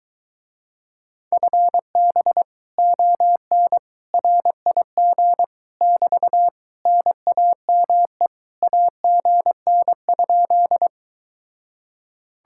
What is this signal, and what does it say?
Morse code, heard as a keyed tone, about 23 wpm, 700 Hz. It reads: FB ON RIG <BT> NAME AGN?